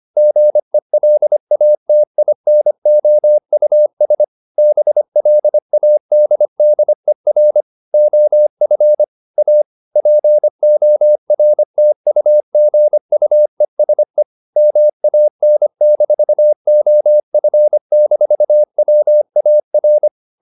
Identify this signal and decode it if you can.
Morse code, heard as a keyed tone, about 25 wpm, 600 Hz. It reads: GELATINOUS BLADDER OF A PORTUGUESE MAN-OF-WAR
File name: the old man and the sea part 1 25wpm